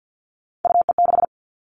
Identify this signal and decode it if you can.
Morse code, heard as a keyed tone, about 48 wpm, 700 Hz. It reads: UE6
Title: All Cut Numbers - 3-Digit 50wpm